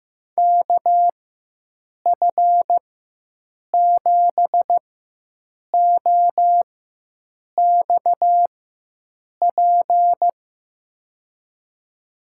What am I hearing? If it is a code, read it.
Morse code, heard as a keyed tone, about 15 wpm, spaced as for 7 wpm, 700 Hz. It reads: KF7OXP
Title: USA Call Signs - 6 Characters - 4x Character Spacing - 15wpm